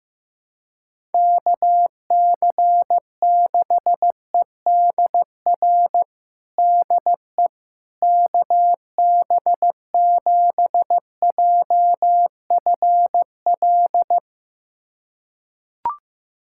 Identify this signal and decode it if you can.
Morse code, heard as a keyed tone, about 15 wpm, 700 Hz. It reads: KC6EDR DE KB7JFL
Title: USA Callsign DE Callsign 15wpm